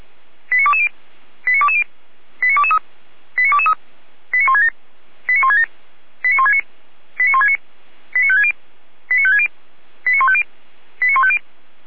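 Five-tone selective call (ZVEI), heard as quick runs of five tones, tones of about 70 ms. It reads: 89220 89220 89223 89223 89167 89167 89169 89169 89560 89560 89150 89150
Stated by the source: ZVEI five-tone call 89123 received by SDR, fire brigade Feuerwehr Orlamünde.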